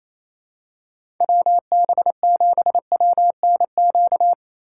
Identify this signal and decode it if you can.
Morse code, heard as a keyed tone, about 28 wpm, 700 Hz. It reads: WB7WNQ
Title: USA Call Signs - 6 Characters - 4x Character Spacing - 28wpm